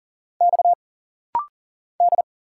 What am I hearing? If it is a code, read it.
Morse code, heard as a keyed tone, about 40 wpm, 700 Hz. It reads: X D
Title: Single Letters - Rapid-Fire 40wpm